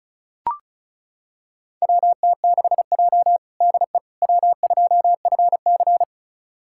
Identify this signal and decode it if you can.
Morse code, heard as a keyed tone, about 36 wpm, 700 Hz. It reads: WT6J DE W2FC
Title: USA Callsign DE Callsign - 4 Characters - 35wpm